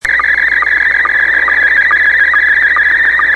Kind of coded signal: SSTV (slow-scan television)